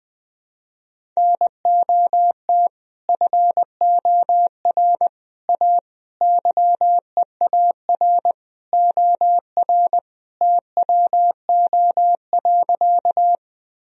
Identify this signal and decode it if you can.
Morse code, heard as a keyed tone, about 21 wpm, 700 Hz. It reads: NOT FOR A YEAR OR TWO.
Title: Sentences from Top 1000 Words - Review Part 1 - 20wpm